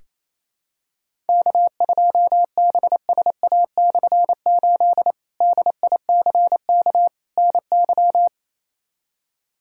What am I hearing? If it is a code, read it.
Morse code, heard as a keyed tone, about 28 wpm, 700 Hz. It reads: K2BSA/8 DICK NY
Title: K1USN SST QSO Practice - 28wpm